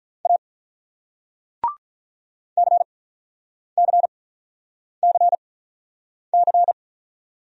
Morse code, heard as a keyed tone, about 45 wpm, 700 Hz. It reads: A C C C C